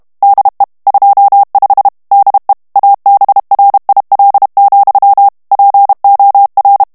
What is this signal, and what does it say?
Morse code, heard as a keyed tone, about 32 wpm, 800 Hz. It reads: DE 25 DE ABRIL, POR
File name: morse mp3 / ESPAGNOL TEXTE EN CLAIR 2 VITESSE 30 MOTS MINUTES